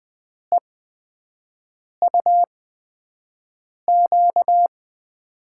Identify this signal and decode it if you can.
Morse code, heard as a keyed tone, about 20 wpm, spaced as for 5 wpm, 700 Hz. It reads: EUQ
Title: USA Call Signs - 8x Character Spacing - 20wpm